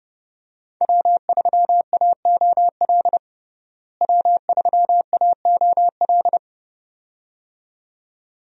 Morse code, heard as a keyed tone, about 30 wpm, 700 Hz. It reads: W3AOL W3AOL
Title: USA Call Signs - 5 Characters - Repeated 2x - 30wpm